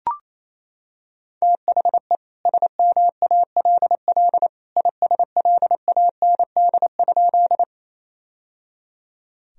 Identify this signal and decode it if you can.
Morse code, heard as a keyed tone, about 28 wpm, 700 Hz. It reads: THE SMALL ISLAND?
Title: Sentences from Top 1000 Words - Review Part 3 - 28wpm